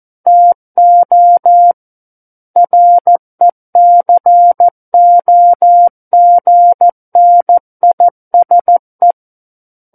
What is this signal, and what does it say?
Morse code, heard as a keyed tone, about 14 wpm, 700 Hz. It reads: TO RECOGNISE
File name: Book One / 05-Chapter 5 - 14 WPM